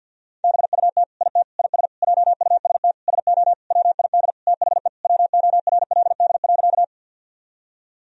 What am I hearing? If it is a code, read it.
Morse code, heard as a keyed tone, about 50 wpm, 700 Hz. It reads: BUT IT IS JUST SO WID THE WORLD.
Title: Sentences from Top 200 Words and Abbreviations 50wpm